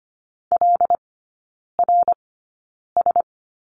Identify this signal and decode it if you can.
Morse code, heard as a keyed tone, about 25 wpm, spaced as for 9 wpm, 700 Hz. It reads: LRS
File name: Lesson 15e - Number 5 - Three Characters 25wpm